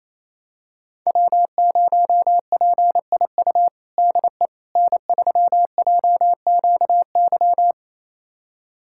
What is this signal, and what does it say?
Morse code, heard as a keyed tone, about 28 wpm, 700 Hz. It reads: W0PIU DE N3JQY
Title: USA Callsign DE Callsign - 5 Characters - 28wpm